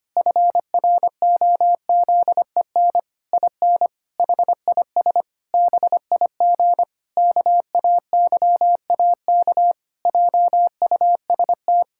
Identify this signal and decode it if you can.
Morse code, heard as a keyed tone, about 25 wpm, 700 Hz. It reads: FROZEN IN HIS BIG KAYAK JUST
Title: Pangram Challenge v2 - 25wpm